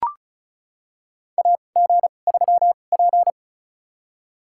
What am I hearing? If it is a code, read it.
Morse code, heard as a keyed tone, about 34 wpm, spaced as for 25 wpm, 700 Hz. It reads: AG3P
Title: USA Call Signs - 2x Character Spacing - 35wpm